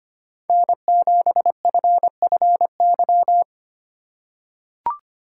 Morse code, heard as a keyed tone, about 25 wpm, 700 Hz. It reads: N7FFY
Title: USA Call Signs - 2x Character Spacing - 25wpm